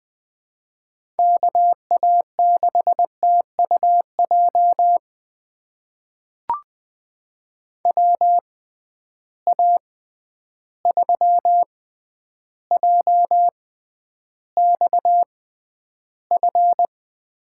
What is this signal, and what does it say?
Morse code, heard as a keyed tone, about 20 wpm, spaced as for 10 wpm, 700 Hz. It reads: KA6TUJ WA3JXF
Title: USA Call Signs - 6 Characters - 6x Character Spacing - 20wpm